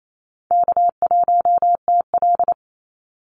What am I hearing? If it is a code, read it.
Morse code, heard as a keyed tone, about 28 wpm, 700 Hz. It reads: K1TL
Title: Easy Call Signs Repeated 2x - 28wpm